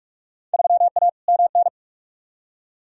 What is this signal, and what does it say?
Morse code, heard as a keyed tone, about 46 wpm, 700 Hz. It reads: 3A MN
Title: Field Day Exchanges - 45wpm